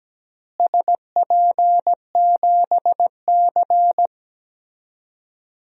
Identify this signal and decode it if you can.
Morse code, heard as a keyed tone, about 17 wpm, 700 Hz. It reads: SP7C
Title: DXCC-100 Call Signs - 4 Characters - Repeated 3x - 17wpm